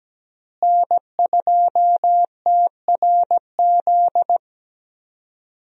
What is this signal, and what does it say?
Morse code, heard as a keyed tone, about 17 wpm, 700 Hz. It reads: N2TRZ
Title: USA Call Signs - 5 Characters - Repeated 2x - 17wpm